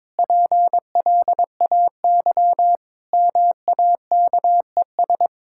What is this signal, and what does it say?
Morse code, heard as a keyed tone, about 22 wpm, 700 Hz. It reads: PLAY MAKES
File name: practice / 22wpm / Sets of 8 Words - 22wpm